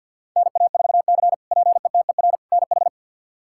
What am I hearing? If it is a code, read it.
Morse code, heard as a keyed tone, about 45 wpm, 700 Hz. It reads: NA4C PETER NH
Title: K1USN SST QSO Practice - 50wpm